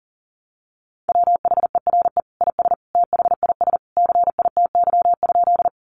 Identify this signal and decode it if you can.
Morse code, heard as a keyed tone, about 40 wpm, 700 Hz. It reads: WHERE IS THIS CITY?
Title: Sentences from Top 300 Words 40wpm